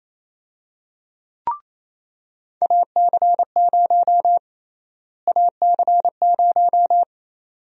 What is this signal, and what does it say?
Morse code, heard as a keyed tone, about 28 wpm, 700 Hz. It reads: AC0 AC0
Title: USA Call Sign Prefixes - Repeated 2x - 28wpm